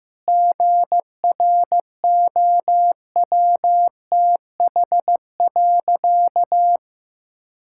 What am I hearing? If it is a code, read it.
Morse code, heard as a keyed tone, about 15 wpm, 700 Hz. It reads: GROWTH.
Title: Sentences from Top 1000 Words - Review Part 5 - 15wpm